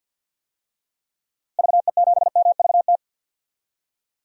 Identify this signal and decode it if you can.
Morse code, heard as a keyed tone, about 51 wpm, 700 Hz. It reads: VE7MVT
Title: DXCC-100 Call Signs - 6 Characters - Repeated 2x - 50wpm